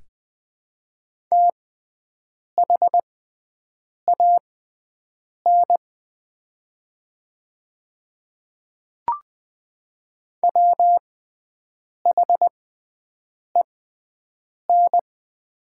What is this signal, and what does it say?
Morse code, heard as a keyed tone, about 20 wpm, spaced as for 7 wpm, 700 Hz. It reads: THAN WHEN
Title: 6x Character Spacing - Top 100 Words - 20wpm